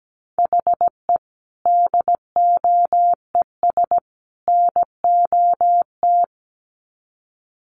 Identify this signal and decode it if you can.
Morse code, heard as a keyed tone, about 17 wpm, 700 Hz. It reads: HE DOES NOT
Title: Sets of 3 Words 17wpm